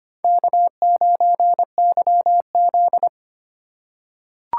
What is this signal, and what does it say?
Morse code, heard as a keyed tone, about 25 wpm, 700 Hz. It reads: K9YZ
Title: USA Call Signs - 4 Characters - 2x Character Spacing - 25wpm